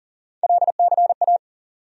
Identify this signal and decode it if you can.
Morse code, heard as a keyed tone, about 40 wpm, 700 Hz. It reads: LCA
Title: USA Call Sign Postfixes - 3 Letters 40wpm